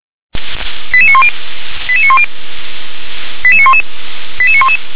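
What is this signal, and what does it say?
Five-tone selective call (ZVEI), heard as quick runs of five tones, tones of about 70 ms. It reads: 88910 88910 88911 88911
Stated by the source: ZVEI five-tone selective call 88912 received by SDR.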